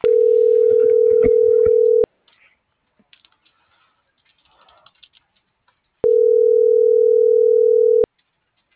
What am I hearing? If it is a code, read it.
Ringback tone (North America), 440 + 480 Hz.